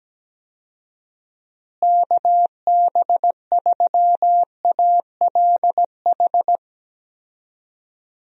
Morse code, heard as a keyed tone, about 17 wpm, 700 Hz. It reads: KB3ALH
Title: USA Call Signs - 6 Characters - Repeated 3x - 17wpm